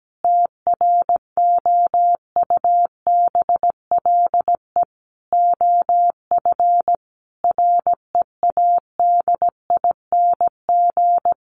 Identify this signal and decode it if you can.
Morse code, heard as a keyed tone, about 17 wpm, 700 Hz. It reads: TROUBLE OF READING
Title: Sentences from Top 1200 Words 17wpm